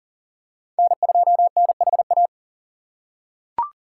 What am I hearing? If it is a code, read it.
Morse code, heard as a keyed tone, about 40 wpm, 700 Hz. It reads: N2NHA
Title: USA Call Signs - 5 Characters - 3x Character Spacing - 40wpm